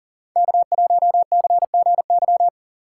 Morse code, heard as a keyed tone, about 49 wpm, 700 Hz. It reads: K1CGY
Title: Speed Racing - USA 1x3 Call Signs 40wpm